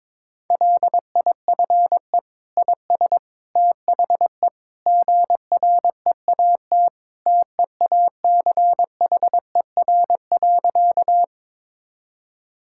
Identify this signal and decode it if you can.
Morse code, heard as a keyed tone, about 22 wpm, 700 Hz. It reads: LIFE IS THE GREAT TEACHER.
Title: Sentences from Top 1000 Words - Review Part 6 - 22wpm